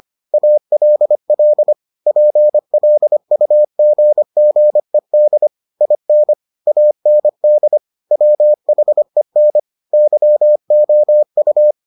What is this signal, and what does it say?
Morse code, heard as a keyed tone, about 25 wpm, 600 Hz. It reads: ALL PLUGGED IN AND WHEN YOU